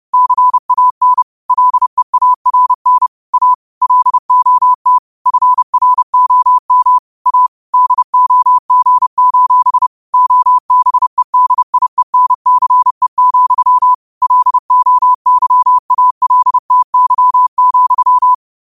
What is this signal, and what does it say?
Morse code, heard as a keyed tone, about 30 wpm, 1000 Hz. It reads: CAN LEARN A LOT FROM A DOG: OBEDIENCE, LOYALTY,
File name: qotdfiles / 30wpm